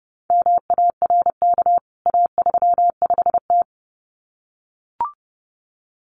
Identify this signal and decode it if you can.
Morse code, heard as a keyed tone, about 30 wpm, 700 Hz. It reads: MARK A35T
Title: CWT Exchange 30wpm